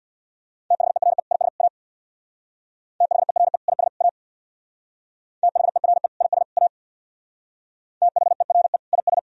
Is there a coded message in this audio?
Yes: Morse code from 0.7 to 9.2 s, heard as a keyed tone, about 64 wpm, 700 Hz. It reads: THERE IS A THERE IS A THERE IS A THERE IS